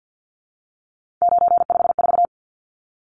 Morse code, heard as a keyed tone, about 50 wpm, 700 Hz. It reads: 954
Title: Numbers - 3-Digit 50wpm